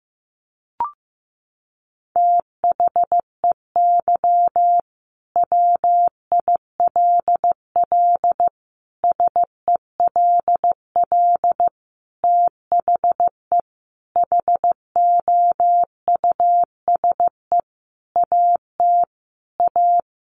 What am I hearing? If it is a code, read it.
Morse code, heard as a keyed tone, about 15 wpm, 700 Hz. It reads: THEY WILL SELL THE HOUSE AT A
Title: Sentences from Top 1200 Words 15wpm